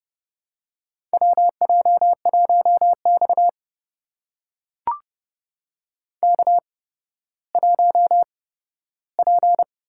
Morse code, heard as a keyed tone, about 29 wpm, spaced as for 14 wpm, 700 Hz. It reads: WJ1X K1P
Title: USA Call Signs - 4 Characters - 8x Character Spacing - 30wpm